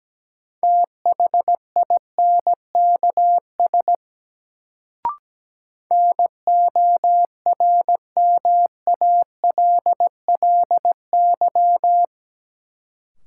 Morse code, heard as a keyed tone, about 17 wpm, 700 Hz. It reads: THINKS NORMALLY
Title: Sentences from Top 1000 Words - Review Part 6 - New Word Drill - 17wpm